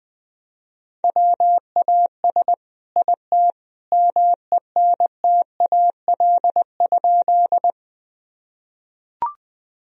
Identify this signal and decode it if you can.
Morse code, heard as a keyed tone, about 20 wpm, 700 Hz. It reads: WAS IT MENTAL?
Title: Sentences from Top 1000 Words - Review Part 4 - 20wpm